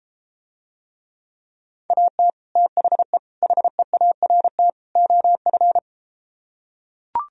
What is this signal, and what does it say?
Morse code, heard as a keyed tone, about 34 wpm, 700 Hz. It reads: AT THE HEART OF